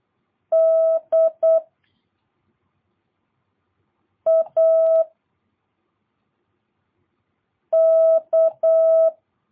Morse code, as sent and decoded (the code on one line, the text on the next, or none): -.. / .- / -.-
D A K